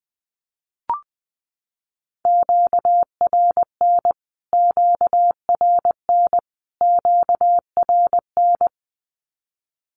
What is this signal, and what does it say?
Morse code, heard as a keyed tone, about 20 wpm, 700 Hz. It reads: QRN QRN QRN
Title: Q-Codes Repeated 3x 20wpm